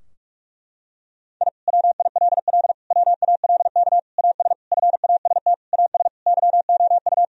Morse code, heard as a keyed tone, about 47 wpm, 700 Hz. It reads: I WILL WALK AS FAST AS YOU
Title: Sentences from Top 1000 Words - Review Part 2 - 45wpm